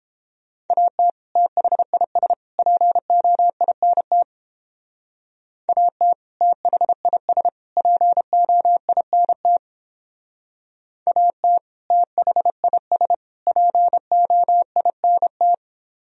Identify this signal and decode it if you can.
Morse code, heard as a keyed tone, about 30 wpm, 700 Hz. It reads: AT THIS POINT AT THIS POINT AT THIS POINT